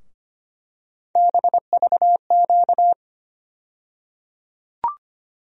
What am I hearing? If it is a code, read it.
Morse code, heard as a keyed tone, about 25 wpm, 700 Hz. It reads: BVQ
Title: USA Call Sign Postfixes - 3 Letters 25wpm